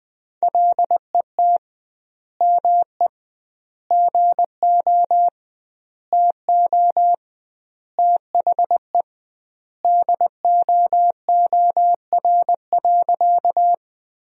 Morse code, heard as a keyed tone, about 21 wpm, 700 Hz. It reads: LET ME GO TO THE DOOR.
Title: 2x Word-Spacing - Sentences from Top 300 Words - 20wpm